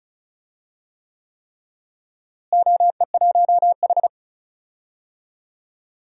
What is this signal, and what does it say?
Morse code, heard as a keyed tone, about 36 wpm, 700 Hz. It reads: OE1H
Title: DXCC-100 Call Signs - 4 Characters - Repeated 3x - 35wpm